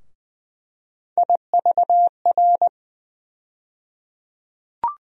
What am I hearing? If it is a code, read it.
Morse code, heard as a keyed tone, about 20 wpm, 700 Hz. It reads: IVR
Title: USA Call Sign Postfixes - 3 Letters 20wpm